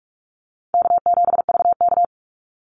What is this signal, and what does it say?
Morse code, heard as a keyed tone, about 44 wpm, 700 Hz. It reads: K7VK
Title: NA SOTA Activator Callsigns 45wpm